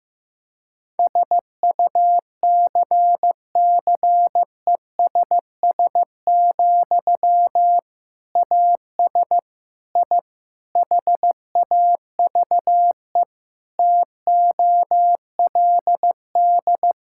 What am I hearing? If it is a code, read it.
Morse code, heard as a keyed tone, about 16 wpm, 700 Hz. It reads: SUCCESS, AS I HAVE TOLD
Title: Sentences from Top 1000 Words - Review Part 6 - 15wpm